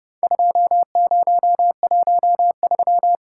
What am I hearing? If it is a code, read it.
Morse code, heard as a keyed tone, about 30 wpm, 700 Hz. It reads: 2013
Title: Worked All Europe DX Contest - QTC Exchange - 30wpm